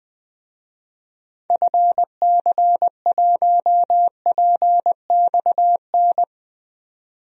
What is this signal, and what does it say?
Morse code, heard as a keyed tone, about 20 wpm, 700 Hz. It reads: FC1PXN
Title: DXCC-100 Call Signs - 6 Characters - 20wpm